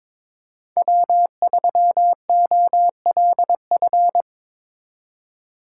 Morse code, heard as a keyed tone, about 25 wpm, 700 Hz. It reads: W3OLF
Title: Speed Racing - USA 1x3 Call Signs 22wpm